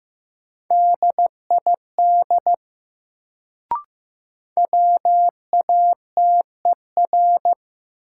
Morse code, heard as a keyed tone, about 15 wpm, 700 Hz. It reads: DID WATER
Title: Sentences from Top 100 Words - New Word Drill - 15wpm